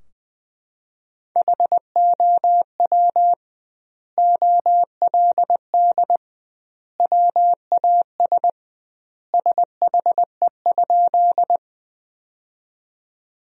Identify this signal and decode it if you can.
Morse code, heard as a keyed tone, about 20 wpm, 700 Hz. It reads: HOW OLD WAS SHE?
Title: 2x Word-Spacing - Sentences from Top 200 Words - 20wpm